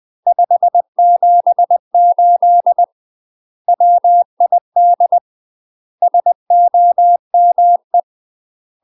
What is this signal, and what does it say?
Morse code, heard as a keyed tone, about 20 wpm, 700 Hz. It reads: 578 WID SOME
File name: UK MASTER QSOs TRAINING / 20WPM / UK MASTER QSOs 0070